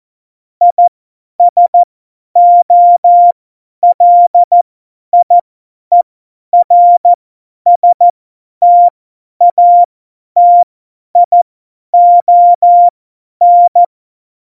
Morse code, heard as a keyed tone, about 14 wpm, spaced as for 10 wpm, 700 Hz. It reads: ISOLIERSTATION
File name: Morsekurs / Witze / Witz 06. Oktober 2020, 50 F70